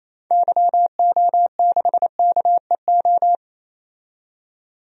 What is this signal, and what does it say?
Morse code, heard as a keyed tone, about 28 wpm, 700 Hz. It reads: YO6KEO
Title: DXCC-100 Call Signs - 6 Characters - 28wpm